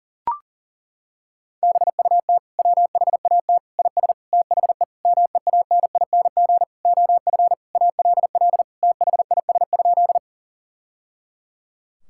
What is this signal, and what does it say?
Morse code, heard as a keyed tone, about 40 wpm, 700 Hz. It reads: BUT WHAT IS THE MEANING OF ALL THIS?
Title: Sentences from Top 1000 Words - Review Part 6 - 40wpm